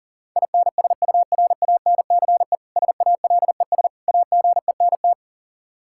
Morse code, heard as a keyed tone, about 40 wpm, 700 Hz. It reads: INSURANCE SALES AGENT
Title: Practice QSO - Exchange 3 - More Info - 40wpm